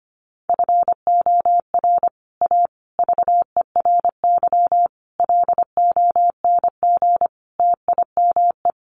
Morse code, heard as a keyed tone, about 25 wpm, 700 Hz. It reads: FOR A VERY LONG TIME
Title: Sets of 5 Words 25wpm